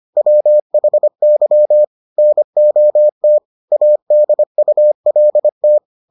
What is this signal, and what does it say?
Morse code, heard as a keyed tone, about 25 wpm, 600 Hz. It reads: WHY NOT ADULT